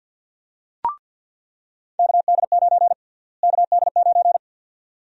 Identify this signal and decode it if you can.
Morse code, heard as a keyed tone, about 47 wpm, 700 Hz. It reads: KD9 KD9
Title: USA Call Sign Prefixes - Repeated 2x - 50wpm